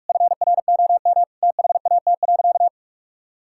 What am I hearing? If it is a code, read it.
Morse code, heard as a keyed tone, about 46 wpm, 700 Hz. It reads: FROM THAT.
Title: Sentences from Top 1000 Words - Review Part 5 - 45wpm